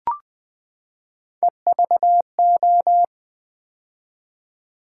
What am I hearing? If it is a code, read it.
Morse code, heard as a keyed tone, about 20 wpm, 700 Hz. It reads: EVO
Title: Build A Word - 3 Letters At A Time - 9-Letter Words - 20wpm